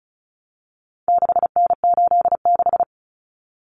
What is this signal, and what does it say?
Morse code, heard as a keyed tone, about 34 wpm, 700 Hz. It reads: 6N86
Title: Common Cut Numbers - 4-Digit 35wpm